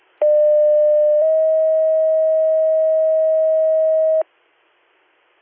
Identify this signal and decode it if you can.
Two-tone page, heard as a one-second tone, then a three-second tone, about 600 then 640 Hz. It is heard from 0.2 to 4.2 s.